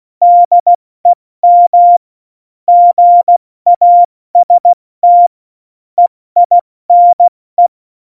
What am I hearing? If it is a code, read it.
Morse code, heard as a keyed tone, about 16 wpm, 700 Hz. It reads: DEM GAST EINE
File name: Morsekurs / Witze / Witz 09. April 2019, 70 F80